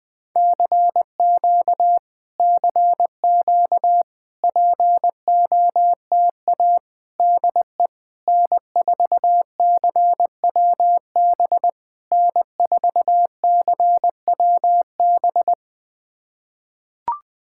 Morse code, heard as a keyed tone, about 20 wpm, 690 Hz. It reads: CQ CQ POTA DE N4CWB N4CWB
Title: POTA QSO Practice - 20wpm